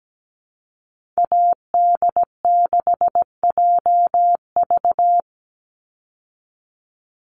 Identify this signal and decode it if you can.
Morse code, heard as a keyed tone, about 17 wpm, 700 Hz. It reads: AD6JV
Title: Call Signs Repeated 3x - 17wpm